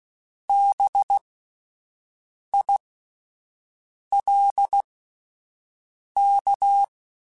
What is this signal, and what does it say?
Morse code, heard as a keyed tone, about 16 wpm, spaced as for 5 wpm, 800 Hz. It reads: BILK